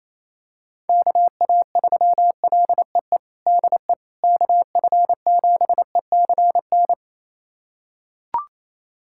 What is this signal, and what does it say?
Morse code, heard as a keyed tone, about 28 wpm, 700 Hz. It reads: KA3LEE DE KF7ECN
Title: USA Callsign DE Callsign - 6 Characters - 28wpm